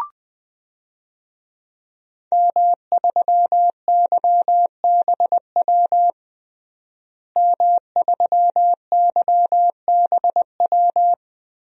Morse code, heard as a keyed tone, about 20 wpm, 700 Hz. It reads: M3YBW M3YBW
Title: DXCC-100 Call Signs - 5 Characters - Repeated 3x - 20wpm